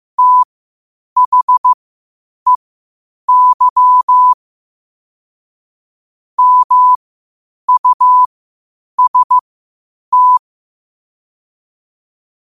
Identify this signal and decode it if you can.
Morse code, heard as a keyed tone, about 15 wpm, spaced as for 8 wpm, 1000 Hz. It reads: THEY MUST
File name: qotdfiles / 8wpm